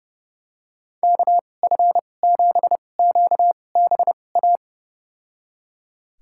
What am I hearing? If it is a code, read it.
Morse code, heard as a keyed tone, about 30 wpm, spaced as for 22 wpm, 700 Hz. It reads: KF7QBA
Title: USA Call Signs - 2x Character Spacing - 30wpm